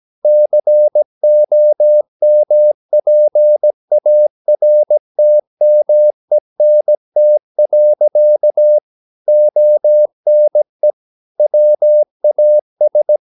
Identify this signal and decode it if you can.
Morse code, heard as a keyed tone, about 17 wpm, 600 Hz. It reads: COMPARTMENT. ONE WAS
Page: Romanen Never af Ken Follet på engelsk.